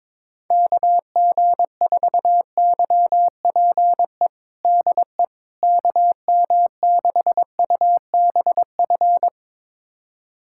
Morse code, heard as a keyed tone, about 22 wpm, 700 Hz. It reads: KG4YPE DE KM6UBF
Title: USA Callsign DE Callsign 22wpm